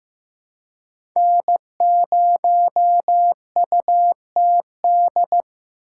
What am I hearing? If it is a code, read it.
Morse code, heard as a keyed tone, about 18 wpm, 700 Hz. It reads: N0UTD
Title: Speed Racing - USA 1x3 Call Signs 15wpm